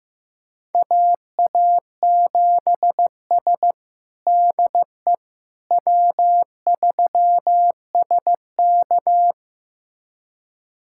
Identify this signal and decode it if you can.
Morse code, heard as a keyed tone, about 15 wpm, 700 Hz. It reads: AA7S DE W3SK
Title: USA Callsign DE Callsign - 4 Characters - 15wpm